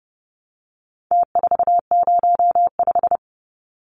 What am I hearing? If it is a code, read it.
Morse code, heard as a keyed tone, about 30 wpm, 700 Hz. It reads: T405
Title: Numbers - Contest Style - 4-Digit 30wpm